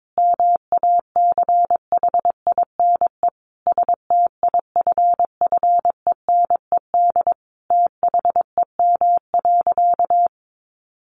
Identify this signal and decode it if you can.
Morse code, heard as a keyed tone, about 22 wpm, 700 Hz. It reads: MACHINE STIFFENED THEM.
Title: A Sound Of Thunder 22wpm